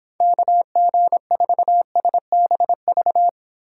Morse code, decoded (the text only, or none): KG4SBV